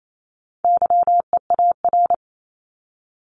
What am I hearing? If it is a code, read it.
Morse code, heard as a keyed tone, about 28 wpm, 700 Hz. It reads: YEAR
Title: ES:S Suffix - Top 100 Words with Base Words 28wpm